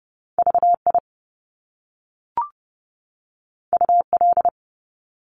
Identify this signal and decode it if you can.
Morse code, heard as a keyed tone, about 30 wpm, 700 Hz. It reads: VI UL
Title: Top Two Letter Combinations in Words 30wpm